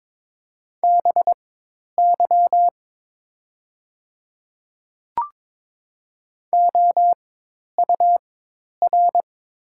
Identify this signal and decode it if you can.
Morse code, heard as a keyed tone, about 22 wpm, spaced as for 10 wpm, 700 Hz. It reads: BY OUR
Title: 4x Character Spacing - Top 200 Words - 22wpm